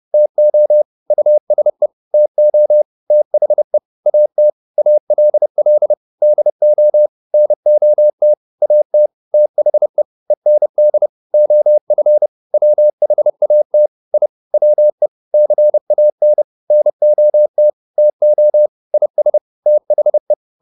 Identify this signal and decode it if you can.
Morse code, heard as a keyed tone, about 30 wpm, 600 Hz. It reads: TO USE TO THE AT ALL DO NOT AT THE END OF WHAT I WE CAN NOT TO IS THE